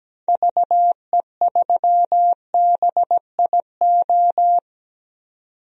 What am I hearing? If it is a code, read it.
Morse code, heard as a keyed tone, about 17 wpm, 700 Hz. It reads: VE3BIO
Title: DXCC-100 Call Signs - 6 Characters - 17wpm